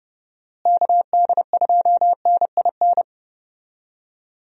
Morse code, heard as a keyed tone, about 30 wpm, 700 Hz. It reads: KD2NIN
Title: USA Call Signs - 6 Characters - Repeated 3x - 30wpm